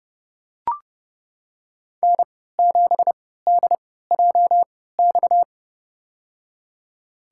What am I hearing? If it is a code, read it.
Morse code, heard as a keyed tone, about 30 wpm, spaced as for 17 wpm, 700 Hz. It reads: N7DJX
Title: USA Call Signs - 5 Characters - 3x Character Spacing - 30wpm